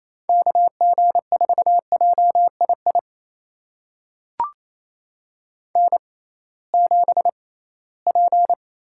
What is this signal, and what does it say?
Morse code, heard as a keyed tone, about 28 wpm, spaced as for 18 wpm, 700 Hz. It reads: KG4JII N7P
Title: USA Call Signs - 6x Character Spacing - 28wpm